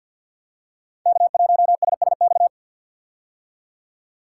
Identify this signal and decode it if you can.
Morse code, heard as a keyed tone, about 50 wpm, 690 Hz. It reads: K1SSX
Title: Code Groups - 5 Letter:Numbers - 50wpm